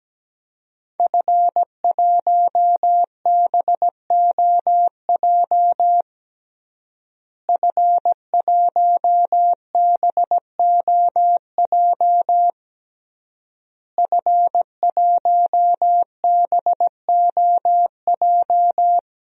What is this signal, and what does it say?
Morse code, heard as a keyed tone, about 17 wpm, 700 Hz. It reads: F1BOJ F1BOJ F1BOJ
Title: DXCC-100 Call Signs - 5 Characters - Repeated 3x - 17wpm